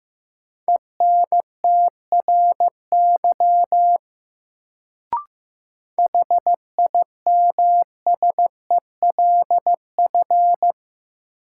Morse code, heard as a keyed tone, about 16 wpm, 700 Hz. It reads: ENTRY HIMSELF
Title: Sentences from Top 1500 Words - New Word Drill Alt - 15wpm